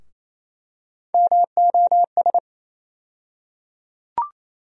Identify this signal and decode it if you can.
Morse code, heard as a keyed tone, about 28 wpm, 700 Hz. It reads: MOS
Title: USA Call Sign Postfixes - 3 Letters 28wpm